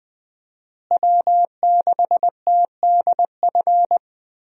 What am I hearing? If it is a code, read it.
Morse code, heard as a keyed tone, about 20 wpm, 700 Hz. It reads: W6TDF
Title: USA Call Signs - 5 Characters - Repeated 3x - 20wpm